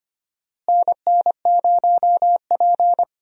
Morse code, heard as a keyed tone, about 25 wpm, 700 Hz. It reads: NN0P
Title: USA Call Signs - 4 Characters - Repeated 3x - 25wpm